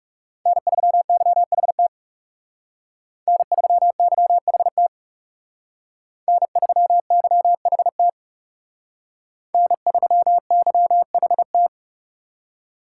Morse code, heard as a keyed tone, about 38 wpm, 700 Hz. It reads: N3YHT N3YHT N3YHT N3YHT